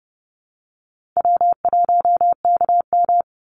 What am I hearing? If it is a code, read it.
Morse code, heard as a keyed tone, about 30 wpm, 700 Hz. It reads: W1KM
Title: Call Signs Repeated 2x - 30wpm